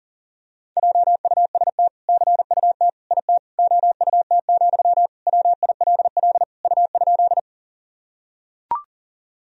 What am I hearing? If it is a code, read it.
Morse code, heard as a keyed tone, about 40 wpm, 700 Hz. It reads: JUST CUT IT OUT, WILL U?
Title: Sentences from Top 300 Words and Abbreviations 40wpm